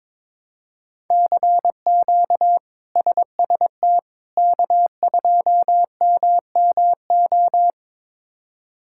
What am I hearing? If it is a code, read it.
Morse code, heard as a keyed tone, about 21 wpm, 700 Hz. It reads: CQ SST K2MMO
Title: K1USN SST QSO Practice - 22wpm